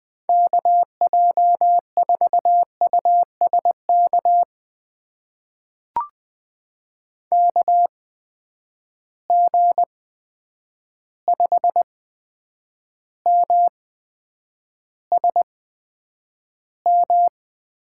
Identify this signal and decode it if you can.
Morse code, heard as a keyed tone, about 20 wpm, spaced as for 9 wpm, 700 Hz. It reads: KJ4USK KG5MSM
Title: USA Call Signs - 6 Characters - 8x Character Spacing - 20wpm